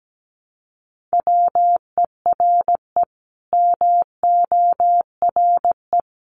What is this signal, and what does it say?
Morse code, heard as a keyed tone, about 17 wpm, 700 Hz. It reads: WERE MORE
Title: Sets of 5 Words Encore2 17wpm